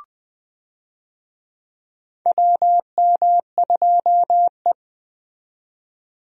Morse code, heard as a keyed tone, about 20 wpm, 700 Hz. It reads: WM2E
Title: USA Call Signs - 4 Characters - 20wpm